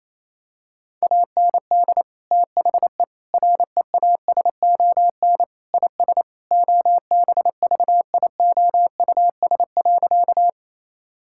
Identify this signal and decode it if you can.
Morse code, heard as a keyed tone, about 28 wpm, 700 Hz. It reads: AND THE REASON IS OBVIOUS.
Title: Sentences from Top 1600 Words 28wpm